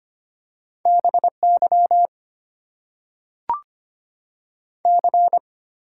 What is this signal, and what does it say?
Morse code, heard as a keyed tone, about 25 wpm, 690 Hz. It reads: BY C
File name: Top 90 QSO Elements - 8x Character Spacing - 25wpm